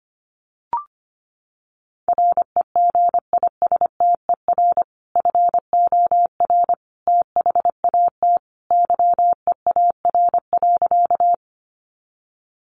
Morse code, heard as a keyed tone, about 25 wpm, 700 Hz. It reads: REGISTER FOR THAT YEAR.
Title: Sentences from Top 1200 Words 25wpm